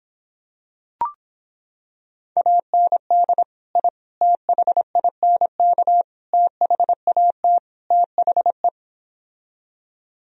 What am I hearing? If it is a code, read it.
Morse code, heard as a keyed tone, about 26 wpm, 700 Hz. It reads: AND I THINK THAT THE